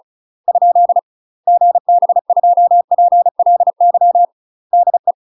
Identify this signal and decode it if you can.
Morse code, heard as a keyed tone, about 35 wpm, 700 Hz. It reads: ? GB2PLY DE
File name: UK MASTER QSOs TRAINING / 35WPM / UK MASTER QSOs 0016